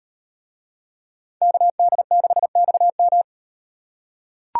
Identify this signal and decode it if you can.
Morse code, heard as a keyed tone, about 38 wpm, 700 Hz. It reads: KD6XM